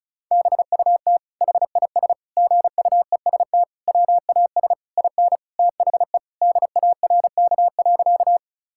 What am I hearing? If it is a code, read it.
Morse code, heard as a keyed tone, about 36 wpm, 700 Hz. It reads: BUT HIS GUEST WAS IN THE DARK.
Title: Sentences from Top 1600 Words 35wpm